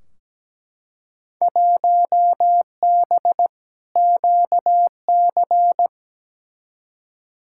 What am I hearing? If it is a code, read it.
Morse code, heard as a keyed tone, about 17 wpm, 700 Hz. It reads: 1B QC
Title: Field Day Exchanges - 17wpm